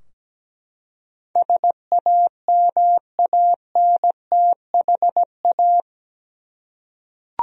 Morse code, heard as a keyed tone, about 17 wpm, 700 Hz. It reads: SAMANTHA
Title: USA Common Names - 17wpm